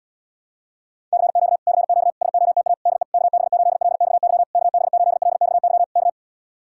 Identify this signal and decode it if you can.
Morse code, heard as a keyed tone, about 66 wpm, 700 Hz. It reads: CQ CQ SOTA DE KB8LXY KB8LXY K